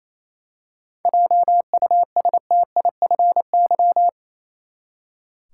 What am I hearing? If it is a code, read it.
Morse code, heard as a keyed tone, about 28 wpm, 700 Hz. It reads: JUSTIFY
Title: Seven Letter Words - 28wpm